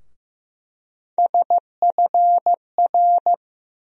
Morse code, heard as a keyed tone, about 15 wpm, 700 Hz. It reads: SFR
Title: USA Call Sign Postfixes - 3 Letters 15wpm